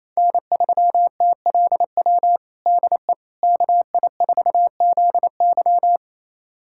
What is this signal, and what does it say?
Morse code, heard as a keyed tone, about 28 wpm, 700 Hz. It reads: N3TLW DE KI4ZY
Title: USA Callsign DE Callsign - 5 Characters - 28wpm